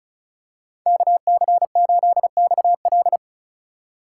Morse code, heard as a keyed tone, about 36 wpm, 700 Hz. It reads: KC8XL
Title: Code Groups - 5 Letter:Numbers - 35wpm